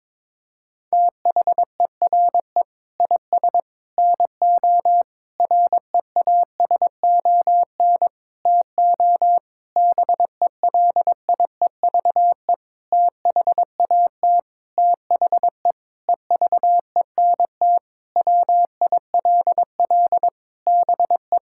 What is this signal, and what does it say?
Morse code, heard as a keyed tone, about 22 wpm, 700 Hz. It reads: THERE IS NO REASON TO BELIEVE THAT THE EVENT WILL BE
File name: practice / 22wpm / Sentences with 8-Word Phrases - 22wpm